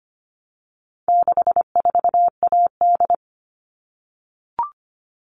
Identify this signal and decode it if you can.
Morse code, heard as a keyed tone, about 25 wpm, 700 Hz. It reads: 64AD
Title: All Cut Numbers - 4-Digit 25wpm